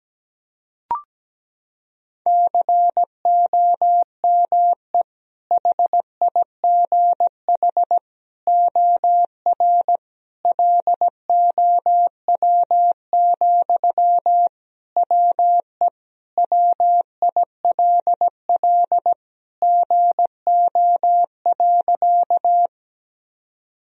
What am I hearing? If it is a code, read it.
Morse code, heard as a keyed tone, about 17 wpm, 700 Hz. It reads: COME HIGH OR LOW, WE WILL GO.
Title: Sentences from Top 1000 Words - Review Part 1 - 17wpm